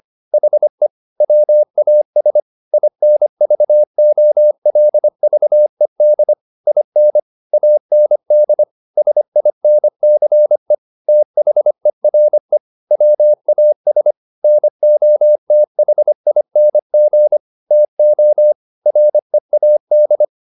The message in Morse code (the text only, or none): HE WAS INVOLVED IN AND SINCE THERE WAS NOTHING TO READ